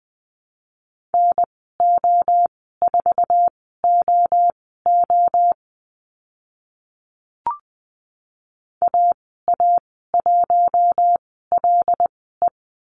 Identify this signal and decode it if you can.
Morse code, heard as a keyed tone, about 20 wpm, spaced as for 14 wpm, 700 Hz. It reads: NO4OO AA1LE
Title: 076 Lesson 13g - Letter L - Call Signs 20wpm